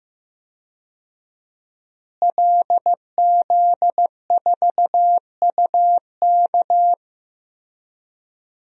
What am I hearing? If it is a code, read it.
Morse code, heard as a keyed tone, about 15 wpm, 700 Hz. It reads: LZ4UK
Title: DXCC-100 Call Signs - 5 Characters - Repeated 3x - 15wpm